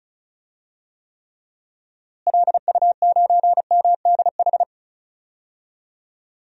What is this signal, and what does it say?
Morse code, heard as a keyed tone, about 34 wpm, 700 Hz. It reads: LU9MDH
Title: DXCC-100 Call Signs - 6 Characters - Repeated 2x - 35wpm